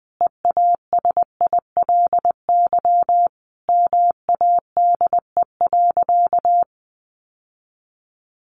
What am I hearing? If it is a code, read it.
Morse code, heard as a keyed tone, about 20 wpm, 700 Hz. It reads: EASILY MADE.
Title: Sentences from Top 1200 Words 20wpm